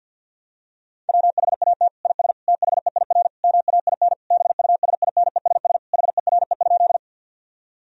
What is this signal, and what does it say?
Morse code, heard as a keyed tone, about 49 wpm, 700 Hz. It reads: WHAT IS THEIR MAIN BUSINESS HERE?
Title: Sentences from Top 1000 Words - Review Part 2 - 50wpm